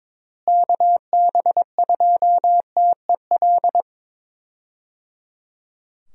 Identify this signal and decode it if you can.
Morse code, heard as a keyed tone, about 21 wpm, 700 Hz. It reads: KB2TEL
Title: USA Call Signs - 6 Characters - Repeated 3x - 22wpm